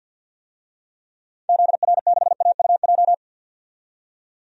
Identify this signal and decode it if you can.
Morse code, heard as a keyed tone, about 48 wpm, 700 Hz. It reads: ZR6AUJ
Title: DXCC-100 Call Signs - 6 Characters - 50wpm